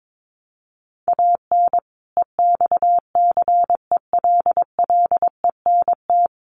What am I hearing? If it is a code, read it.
Morse code, heard as a keyed tone, about 22 wpm, 700 Hz. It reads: AN EXCELLENT
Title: Sentences from Top 1200 Words 22wpm